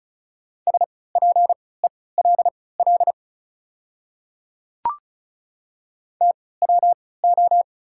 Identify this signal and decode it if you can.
Morse code, heard as a keyed tone, about 34 wpm, spaced as for 20 wpm, 690 Hz. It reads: SPELL TWO
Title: 3x Character Spacing - Top 200 Words - 35wpm